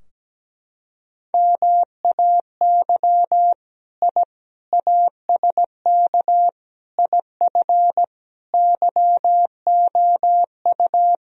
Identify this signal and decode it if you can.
Morse code, heard as a keyed tone, about 17 wpm, 700 Hz. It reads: MAY I ASK IF YOU
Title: Sentences from Top 1000 Words - Review Part 3 - 17wpm